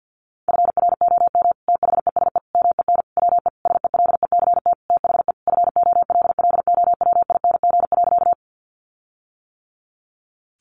Morse code, heard as a keyed tone, about 50 wpm, 700 Hz. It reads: FROM THESE MEN WE SELECT THE FOLLOWING.
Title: Sentences from Top 1300 Words 50wpm